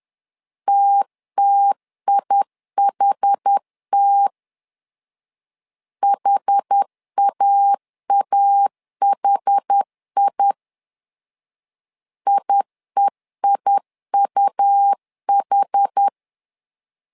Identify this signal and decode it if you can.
Morse code, heard as a keyed tone, about 10 wpm, 800 Hz. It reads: TTIHT HAAHI IEIUH
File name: LEZIONI / 2 UHDN / 2 UH 10wpm 1714